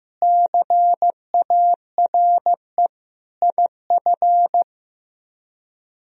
Morse code, heard as a keyed tone, about 15 wpm, 700 Hz. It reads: CARE IF
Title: Speed-Racing - Sets of 5 Words 15wpm